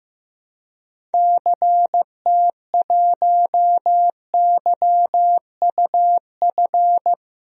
Morse code, heard as a keyed tone, about 15 wpm, 700 Hz. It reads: CT1YUF
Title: DXCC-100 Call Signs - 6 Characters - Repeated 3x - 15wpm